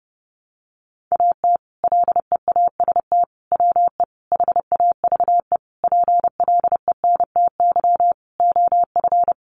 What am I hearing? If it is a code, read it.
Morse code, heard as a keyed tone, about 30 wpm, 700 Hz. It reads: AT LEAST WE HAVE PLENTY OF
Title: Sentences from Top 1300 Words 30wpm